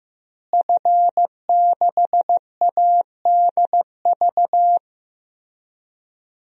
Morse code, heard as a keyed tone, about 15 wpm, 700 Hz. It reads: F6ADV
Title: DXCC-100 Call Signs - 5 Characters - Repeated 3x - 15wpm